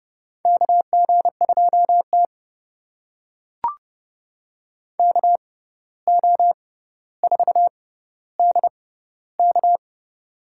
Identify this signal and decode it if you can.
Morse code, heard as a keyed tone, about 29 wpm, spaced as for 14 wpm, 700 Hz. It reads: KG2T KO4DK
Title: USA Call Signs - 6x Character Spacing - 30wpm